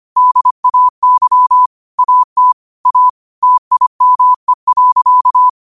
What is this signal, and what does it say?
Morse code, heard as a keyed tone, about 25 wpm, 1000 Hz. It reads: DAY AT A TIME.
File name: 25wpm